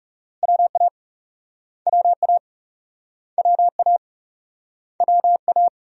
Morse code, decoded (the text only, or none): WA WA WA WA